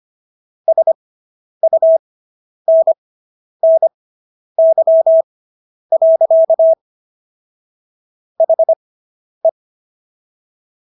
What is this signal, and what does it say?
Morse code, heard as a keyed tone, about 25 wpm, spaced as for 10 wpm, 650 Hz. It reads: SUNNY. HE